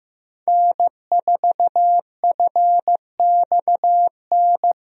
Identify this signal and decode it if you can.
Morse code, heard as a keyed tone, about 15 wpm, 700 Hz. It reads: N4FXN
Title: USA Call Signs - 5 Characters - Repeated 2x - 15wpm